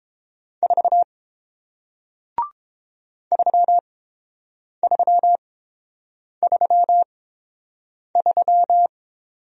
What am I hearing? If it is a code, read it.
Morse code, heard as a keyed tone, about 30 wpm, spaced as for 7 wpm, 700 Hz. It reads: 4 3333